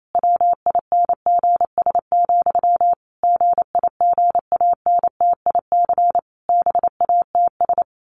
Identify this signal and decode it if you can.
Morse code, heard as a keyed tone, about 28 wpm, 700 Hz. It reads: WINGS, GIGANTIC BATS
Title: A Sound Of Thunder 28wpm